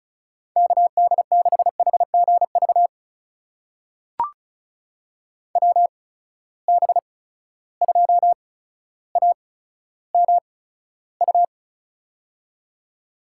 Morse code, heard as a keyed tone, about 36 wpm, spaced as for 15 wpm, 700 Hz. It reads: KD6HGV WB2AMU
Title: USA Call Signs - 6 Characters - 8x Character Spacing - 35wpm